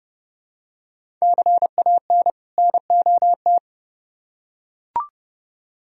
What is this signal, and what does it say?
Morse code, heard as a keyed tone, about 29 wpm, 700 Hz. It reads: CAN NOT